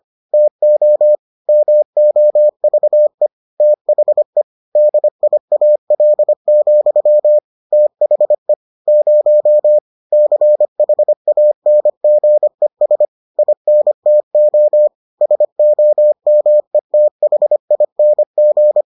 Morse code, heard as a keyed tone, about 25 wpm, 600 Hz. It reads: TO MOVE THE DIAL, THE 0 CHANGES INTO SOMETHING